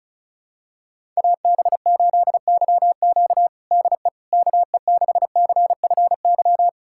Morse code, heard as a keyed tone, about 36 wpm, 700 Hz. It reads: AB8YQ DE KE6CFY
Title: USA Callsign DE Callsign 35wpm